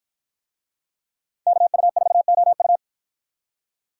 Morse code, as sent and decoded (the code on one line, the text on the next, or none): -.- ..- ....- --- ..-
KU4OU